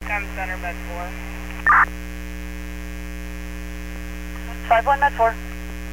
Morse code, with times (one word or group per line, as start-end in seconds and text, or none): none